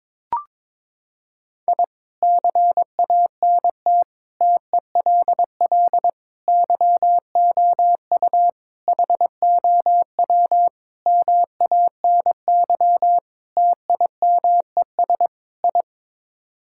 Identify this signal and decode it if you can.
Morse code, heard as a keyed tone, about 22 wpm, 700 Hz. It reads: I CANT TELL YOU HOW MANY TIMES I
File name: practice / 22wpm / Sets of 8 Words - 22wpm